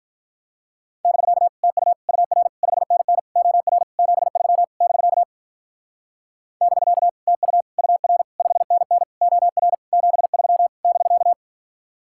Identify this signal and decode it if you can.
Morse code, heard as a keyed tone, about 44 wpm, 700 Hz. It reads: <BK> TU UR 5NN OR 73 <BK> <BK> TU UR 5NN OR 73 <BK>
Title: Speed-Racing Common QSO Phrases - 35wpm